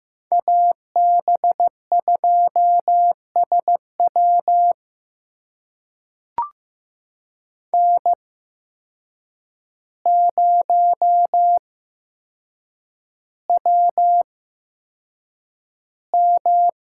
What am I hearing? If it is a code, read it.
Morse code, heard as a keyed tone, about 15 wpm, spaced as for 7 wpm, 700 Hz. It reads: AB2SW N0WM
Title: USA Call Signs - 5 Characters - 8x Character Spacing - 15wpm